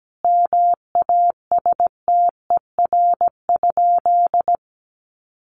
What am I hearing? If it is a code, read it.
Morse code, heard as a keyed tone, about 17 wpm, 700 Hz. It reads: MASTER?
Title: Sentences from Top 1300 Words 17wpm